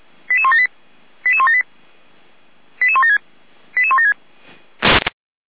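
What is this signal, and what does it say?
Five-tone selective call (ZVEI), heard as quick runs of five tones, tones of about 70 ms. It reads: 88178 88178 88176 88176
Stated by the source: Five-tone ZVEI selective call received on an SDR.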